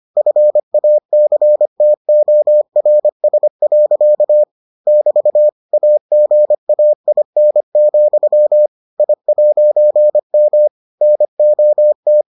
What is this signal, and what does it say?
Morse code, heard as a keyed tone, about 25 wpm, 600 Hz. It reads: FACTORS. <BT> AGAIN, I'M NOT